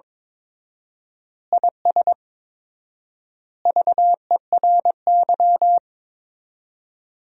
Morse code, heard as a keyed tone, about 22 wpm, 700 Hz. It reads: IS VERY
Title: 4x Word-Spacing - Sentences from Top 500 Words - 22wpm